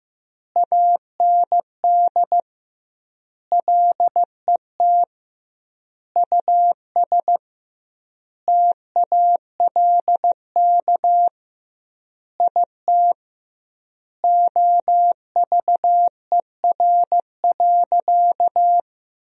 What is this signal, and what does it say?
Morse code, heard as a keyed tone, about 16 wpm, 700 Hz. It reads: AND LET US TALK IT OVER.
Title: 2x Word-Spacing - Sentences from Top 500 Words - 15wpm